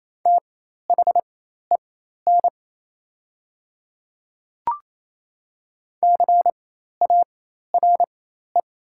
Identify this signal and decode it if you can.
Morse code, heard as a keyed tone, about 28 wpm, spaced as for 13 wpm, 700 Hz. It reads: THEN CARE
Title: 4x Character Spacing - Top 300 Words - 28wpm